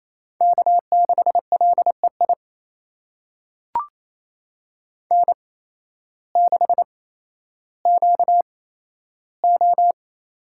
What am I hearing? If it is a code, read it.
Morse code, heard as a keyed tone, about 28 wpm, spaced as for 13 wpm, 700 Hz. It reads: K6LEI N6QO
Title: USA Call Signs - 5 Characters - 8x Character Spacing - 28wpm